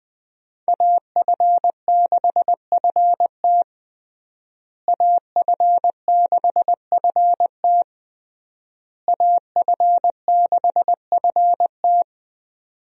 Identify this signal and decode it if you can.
Morse code, heard as a keyed tone, about 20 wpm, 700 Hz. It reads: AF6FT AF6FT AF6FT
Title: USA Call Signs - 5 Characters - Repeated 3x - 20wpm